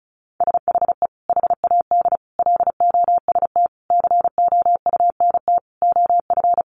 Morse code, heard as a keyed tone, about 34 wpm, 700 Hz. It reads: SHE HAD LOST COUNT OF
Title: Sentences from Top 1500 Words 35wpm